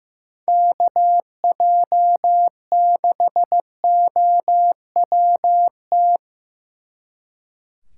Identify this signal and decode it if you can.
Morse code, heard as a keyed tone, about 16 wpm, 700 Hz. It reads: KJ6OWT
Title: Speed Racing - USA 2x3 Call Signs 15wpm